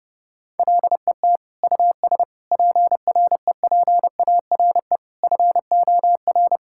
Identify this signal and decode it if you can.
Morse code, heard as a keyed tone, about 31 wpm, 700 Hz. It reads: LET US PREPARE FOR
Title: Sentences from Top 1000 Words - Review Part 6 - 30wpm